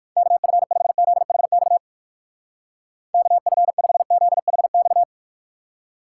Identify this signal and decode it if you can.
Morse code, heard as a keyed tone, about 56 wpm, 700 Hz. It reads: KF5ZHX KF5ZHX